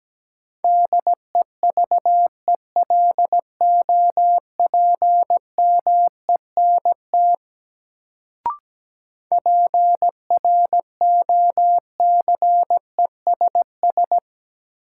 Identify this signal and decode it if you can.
Morse code, heard as a keyed tone, about 17 wpm, 700 Hz. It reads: DEVELOPMENT PROCESS
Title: Sets of 3 Words - New Word Drill - 17wpm